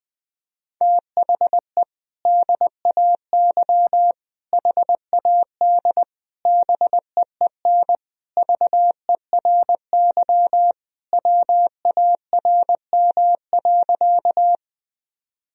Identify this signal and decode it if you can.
Morse code, heard as a keyed tone, about 20 wpm, 700 Hz. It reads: THE DAY HAD BEEN VERY WARM.
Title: Sentences from Top 1000 Words - Review Part 2 - 20wpm